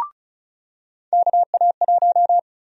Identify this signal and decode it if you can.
Morse code, heard as a keyed tone, about 36 wpm, 700 Hz. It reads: KA1
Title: USA Call Sign Prefixes - 35wpm